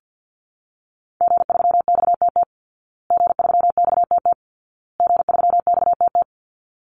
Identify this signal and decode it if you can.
Morse code, heard as a keyed tone, about 50 wpm, 700 Hz. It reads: G3XTT G3XTT G3XTT
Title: Call Signs Repeated 2x - 50wpm